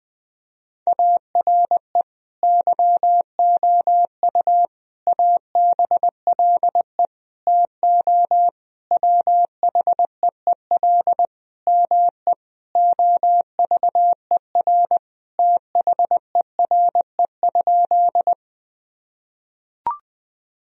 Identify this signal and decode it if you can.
Morse code, heard as a keyed tone, about 21 wpm, 700 Hz. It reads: ARE YOU ABLE TO WHEEL ME OVER THERE?
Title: Sentences from Top 1000 Words - Review Part 3 - 20wpm